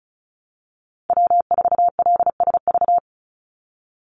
Morse code, heard as a keyed tone, about 34 wpm, 700 Hz. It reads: W4LSV
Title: Call Signs Repeated 3x - 35wpm